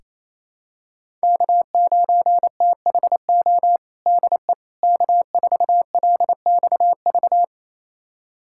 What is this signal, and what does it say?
Morse code, heard as a keyed tone, about 28 wpm, 700 Hz. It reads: K9THO DE K4LXV
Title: USA Callsign DE Callsign 28wpm